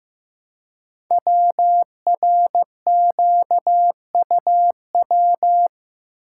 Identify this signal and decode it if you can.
Morse code, heard as a keyed tone, about 15 wpm, 700 Hz. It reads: WRQUW
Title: Code Groups - 5 Letters - 15wpm